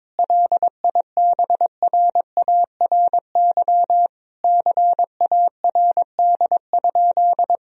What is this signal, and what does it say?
Morse code, heard as a keyed tone, about 21 wpm, 700 Hz. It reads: LIBRARY CARD?
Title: Sentences from Top 1000 Words - Review Part 4 - 22wpm